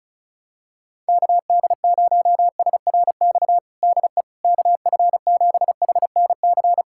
Morse code, heard as a keyed tone, about 36 wpm, 700 Hz. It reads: KD0SRX DE KF7HNC
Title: USA Callsign DE Callsign - 6 Characters - 35wpm